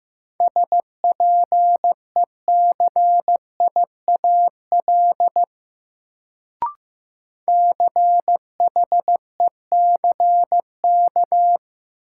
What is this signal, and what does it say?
Morse code, heard as a keyed tone, about 15 wpm, 700 Hz. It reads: SPECIAL CHECK
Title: Sentences from Top 1000 Words - Review Part 3 - New Word Drill - 15wpm